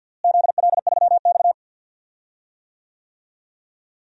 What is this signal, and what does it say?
Morse code, heard as a keyed tone, about 48 wpm, 700 Hz. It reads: ZL3X
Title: DXCC-100 Call Signs - 4 Characters - Repeated 2x - 50wpm